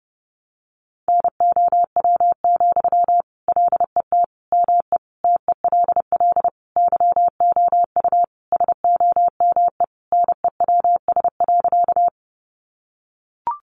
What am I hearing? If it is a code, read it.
Morse code, heard as a keyed tone, about 30 wpm, 700 Hz. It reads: NOW, LET ME TELL YOU SOME NEWS.
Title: Sentences from Top 800 Words 30wpm